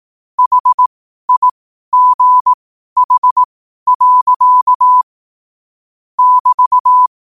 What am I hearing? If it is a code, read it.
Morse code, heard as a keyed tone, about 18 wpm, spaced as for 12 wpm, 1000 Hz. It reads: HIGH. <BT>
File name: qotdfiles / 12wpm